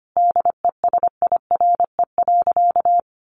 Morse code, heard as a keyed tone, about 25 wpm, 700 Hz. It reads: DESIRE.
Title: Sentences from Top 1100 Words 25wpm